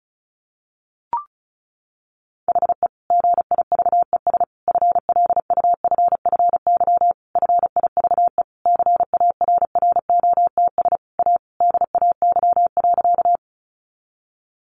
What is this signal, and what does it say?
Morse code, heard as a keyed tone, about 34 wpm, 700 Hz. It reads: HE GIVES FLUFFY FIVE CARROTS A DAY.
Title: Easy Phrases 35wpm